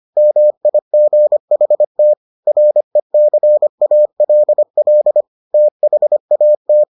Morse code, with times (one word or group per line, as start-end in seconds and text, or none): 0.2-2.1 MIGHT
2.5-5.2 RECALL
5.5-6.8 THAT